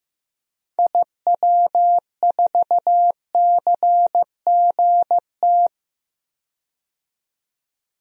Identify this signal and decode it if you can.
Morse code, heard as a keyed tone, about 15 wpm, 700 Hz. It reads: IW4CGT
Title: DXCC-100 Call Signs - 6 Characters - 15wpm